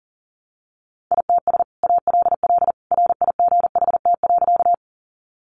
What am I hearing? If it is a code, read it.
Morse code, heard as a keyed tone, about 40 wpm, 700 Hz. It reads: ITS ALL RIGHT.
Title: Sentences from Top 600 Words 40wpm